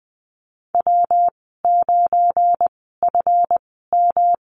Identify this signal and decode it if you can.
Morse code, heard as a keyed tone, about 20 wpm, spaced as for 14 wpm, 700 Hz. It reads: W9FM
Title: 245 Lesson 39g - Letter z - Call Signs 20wpm